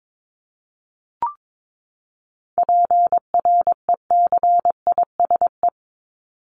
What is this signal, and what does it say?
Morse code, heard as a keyed tone, about 22 wpm, 700 Hz. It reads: PRECISE
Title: LY Suffix - Top 100 Words with Base Words 22wpm